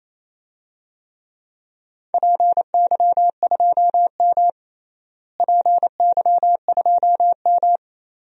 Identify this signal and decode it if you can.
Morse code, heard as a keyed tone, about 28 wpm, 700 Hz. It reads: PY2M PY2M
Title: DXCC-100 Call Signs - 4 Characters - Repeated 2x - 28wpm